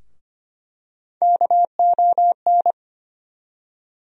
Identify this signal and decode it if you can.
Morse code, heard as a keyed tone, about 25 wpm, 700 Hz. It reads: KON
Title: USA Call Sign Postfixes - 3 Letters 25wpm